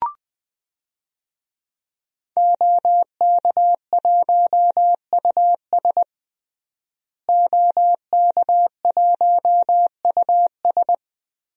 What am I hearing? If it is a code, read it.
Morse code, heard as a keyed tone, about 21 wpm, 700 Hz. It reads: OK1US OK1US
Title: DXCC-100 Call Signs - 5 Characters - Repeated 3x - 20wpm